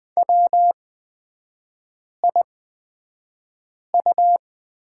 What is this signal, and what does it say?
Morse code, heard as a keyed tone, about 20 wpm, spaced as for 5 wpm, 700 Hz. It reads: WIU
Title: 224 Lesson 36e - BT - Three Characters 20wpm